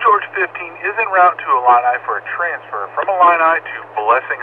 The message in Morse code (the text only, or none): none